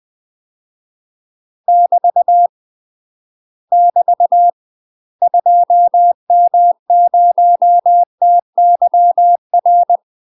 Morse code, heard as a keyed tone, about 20 wpm, 700 Hz. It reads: <BT> <BT> 2M0TYR
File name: UK MASTER QSOs TRAINING / 20WPM / UK MASTER QSOs 0047